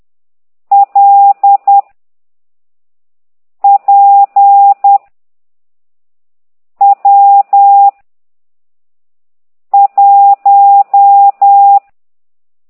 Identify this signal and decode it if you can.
Morse code, heard as a keyed tone, about 10 wpm, 800 Hz. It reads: L P W 1